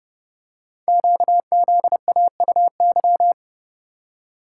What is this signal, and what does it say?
Morse code, heard as a keyed tone, about 28 wpm, 700 Hz. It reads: QZAUY
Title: Code Groups - 5 Letters - 30wpm